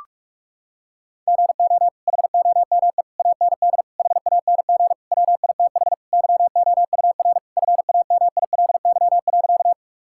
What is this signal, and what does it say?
Morse code, heard as a keyed tone, about 41 wpm, 700 Hz. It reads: GO HOME AND HANG WITH YOUR FAMILY.
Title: Sentences from Top 1600 Words 45wpm